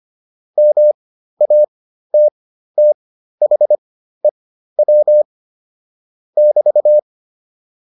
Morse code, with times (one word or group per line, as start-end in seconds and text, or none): 0.6-5.2 MATTHEW
6.4-7.0 <BT>